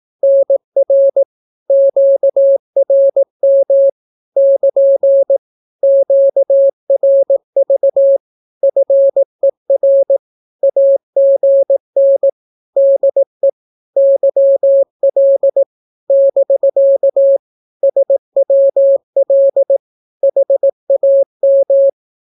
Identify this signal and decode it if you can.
Morse code, heard as a keyed tone, about 18 wpm, 550 Hz. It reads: NR QRM <KN> QRV FER AGN DE YL <BK> SWL HAM
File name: naukapodstaw / skroty / skroty